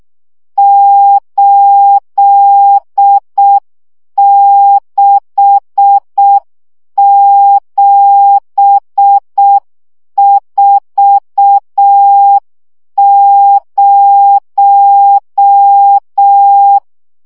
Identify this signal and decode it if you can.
Morse code, heard as a keyed tone, about 6 wpm, 800 Hz. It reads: 86740
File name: morze vizsga 2011-2 5